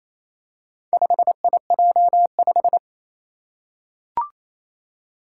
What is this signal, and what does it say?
Morse code, heard as a keyed tone, about 27 wpm, 700 Hz. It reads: 5IJ5
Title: Four Challenging Letter-Number Combinations 28wpm